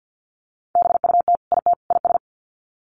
Morse code, heard as a keyed tone, about 50 wpm, 700 Hz. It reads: BUT IT IS
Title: Sets of 3 Words 50wpm